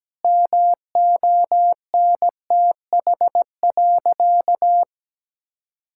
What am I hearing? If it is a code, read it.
Morse code, heard as a keyed tone, about 17 wpm, 700 Hz. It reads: MONTH.
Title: Sentences from Top 1000 Words - Review Part 6 - 17wpm